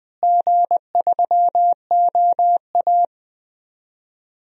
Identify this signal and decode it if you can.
Morse code, heard as a keyed tone, about 20 wpm, 700 Hz. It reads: G3OA
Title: DXCC-100 Call Signs - 4 Characters - Repeated 3x - 20wpm